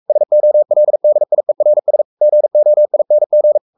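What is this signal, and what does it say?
Morse code, heard as a keyed tone, about 44 wpm, 600 Hz. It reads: SOLDIERS GOING
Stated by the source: Morse code at 43 wpm sending Ken Follett's novel Never in English, from CW med Gnister.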